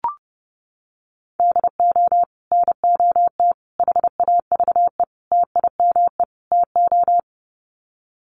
Morse code, heard as a keyed tone, about 30 wpm, 700 Hz. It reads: DO NOT HAVE TIME TO
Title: Sets of 5 Words 30wpm